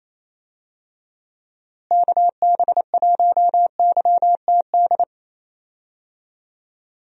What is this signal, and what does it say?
Morse code, heard as a keyed tone, about 28 wpm, 700 Hz. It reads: KB1YTD
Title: USA Call Signs - 6 Characters - Repeated 2x - 28wpm